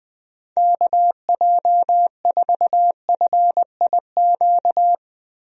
Morse code, heard as a keyed tone, about 20 wpm, 700 Hz. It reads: KJ4FIQ
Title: USA Call Signs - 6 Characters - Repeated 3x - 20wpm